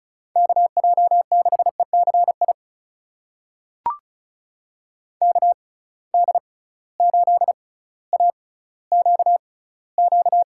Morse code, heard as a keyed tone, about 36 wpm, spaced as for 18 wpm, 700 Hz. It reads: KJ6ECI KD8AQQ
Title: USA Call Signs - 6 Characters - 6x Character Spacing - 35wpm